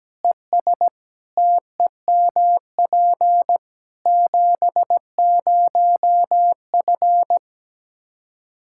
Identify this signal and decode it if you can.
Morse code, heard as a keyed tone, about 17 wpm, 700 Hz. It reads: ES TEMP 70F
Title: Practice QSO - Exchange 2 - Info - 17wpm